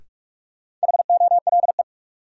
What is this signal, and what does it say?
Morse code, heard as a keyed tone, about 42 wpm, 700 Hz. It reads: HOLE
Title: Sentences from Top 1600 Words - New Word Drill - 45wpm